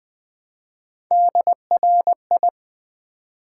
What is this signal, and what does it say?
Morse code, heard as a keyed tone, about 20 wpm, 700 Hz. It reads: DRI
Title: Build A Word - 1 Letter At A Time - 5-Letter Words - 20wpm